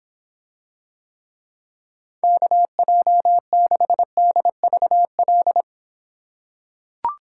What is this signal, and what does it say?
Morse code, heard as a keyed tone, about 26 wpm, 700 Hz. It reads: KJ6DVL